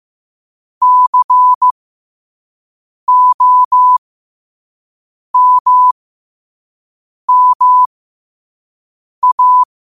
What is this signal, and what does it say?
Morse code, heard as a keyed tone, about 15 wpm, spaced as for 5 wpm, 1000 Hz. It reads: COMMA
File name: qotdfiles / 5wpm